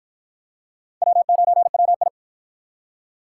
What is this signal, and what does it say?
Morse code, heard as a keyed tone, about 49 wpm, 700 Hz. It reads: W9WI